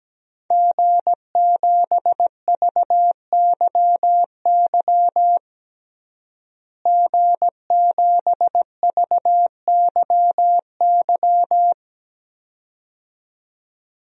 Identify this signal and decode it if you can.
Morse code, heard as a keyed tone, about 17 wpm, 700 Hz. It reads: G7VYY G7VYY
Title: DXCC-100 Call Signs - 5 Characters - Repeated 2x - 17wpm